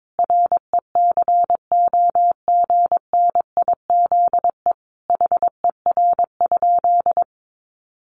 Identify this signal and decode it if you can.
Morse code, heard as a keyed tone, about 22 wpm, 700 Hz. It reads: RECOGNIZE HER?
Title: Sentences from Top 1400 Words 22wpm